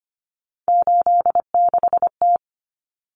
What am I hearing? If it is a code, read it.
Morse code, heard as a keyed tone, about 25 wpm, 700 Hz. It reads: 86T
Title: Common Cut Numbers - 3-Digit 25wpm